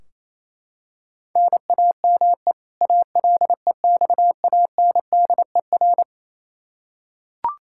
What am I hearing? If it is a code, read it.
Morse code, heard as a keyed tone, about 27 wpm, 700 Hz. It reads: NAME ALEXANDER
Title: Practice QSO - Exchange 1 - Beginning - 28wpm